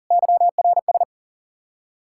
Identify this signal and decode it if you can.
Morse code, heard as a keyed tone, about 52 wpm, 700 Hz. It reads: YRS